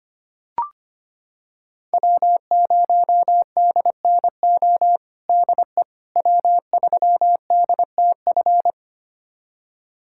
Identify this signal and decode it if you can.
Morse code, heard as a keyed tone, about 25 wpm, 700 Hz. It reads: W0DNO DE W3DTF
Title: USA Callsign DE Callsign - 5 Characters - 25wpm